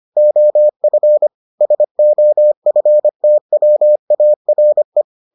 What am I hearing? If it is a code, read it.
Morse code, heard as a keyed tone, about 25 wpm, 600 Hz. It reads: OF SOFTWARE